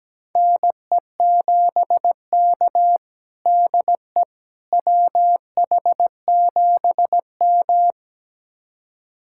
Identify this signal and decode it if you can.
Morse code, heard as a keyed tone, about 17 wpm, 700 Hz. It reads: NE7K DE WH7M
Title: USA Callsign DE Callsign - 4 Characters - 17wpm